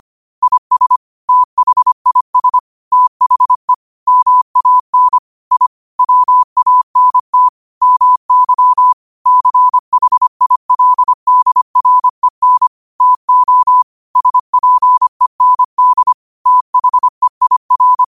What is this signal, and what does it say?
Morse code, heard as a keyed tone, about 25 wpm, 1000 Hz. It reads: IS THIS THE MAN I WANT MY CHILDREN TO SPEND THEIR
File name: qotdfiles / 25wpm